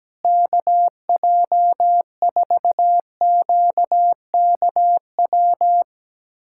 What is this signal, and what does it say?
Morse code, heard as a keyed tone, about 17 wpm, 700 Hz. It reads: KJ4QKW
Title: USA Callsign DE Callsign 17wpm